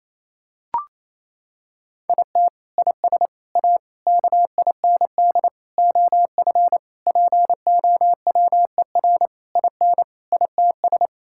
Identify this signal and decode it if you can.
Morse code, heard as a keyed tone, about 28 wpm, 700 Hz. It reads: IT IS A KIND OF POWER IN ITS
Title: Sentences from Top 1000 Words - Review Part 5 - 28wpm